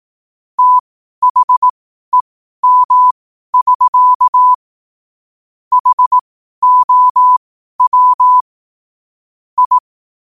Morse code, heard as a keyed tone, about 18 wpm, spaced as for 12 wpm, 1000 Hz. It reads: THEM<SK> HOW I
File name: qotdfiles / 12wpm